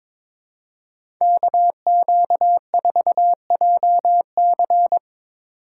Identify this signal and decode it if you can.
Morse code, heard as a keyed tone, about 22 wpm, 700 Hz. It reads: KQ4JC
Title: USA Call Signs - 5 Characters - 2x Character Spacing - 22wpm